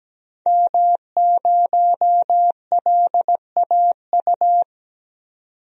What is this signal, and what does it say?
Morse code, heard as a keyed tone, about 17 wpm, 700 Hz. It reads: M0LAU
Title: DXCC-100 Call Signs - 5 Characters - 17wpm